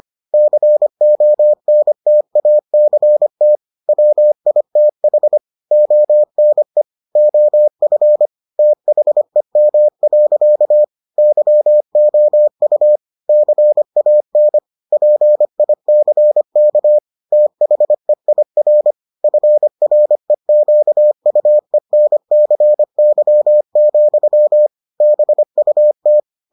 Morse code, heard as a keyed tone, about 25 wpm, 600 Hz. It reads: CONTACT WITH ONE OF THEM. YOU CAN PICK THEIR FREQUENCY, BUT